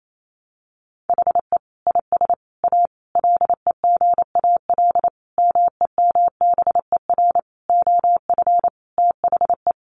Morse code, heard as a keyed tone, about 28 wpm, 700 Hz. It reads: HE IS A LEGAL MEMBER OF THE
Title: Sentences from Top 1200 Words 28wpm